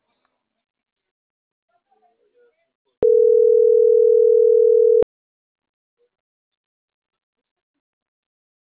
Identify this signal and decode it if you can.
Ringback tone (North America), 440 + 480 Hz.
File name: CENTRAL MAYAB / out atzhyry.gomez